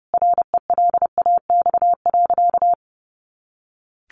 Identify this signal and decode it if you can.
Morse code, heard as a keyed tone, about 30 wpm, 700 Hz. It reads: RELAX.
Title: A Sound Of Thunder 30wpm